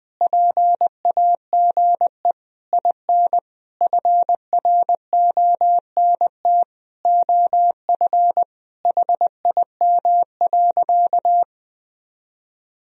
Morse code, heard as a keyed tone, about 20 wpm, 700 Hz. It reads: PAGE IN FRONT OF HIM.
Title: Sentences from Top 1000 Words - Review Part 1 - 20wpm